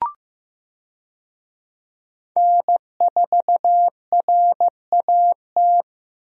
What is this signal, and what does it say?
Morse code, heard as a keyed tone, about 16 wpm, 700 Hz. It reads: N4RAT
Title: USA Call Signs - 5 Characters - Repeated 2x - 15wpm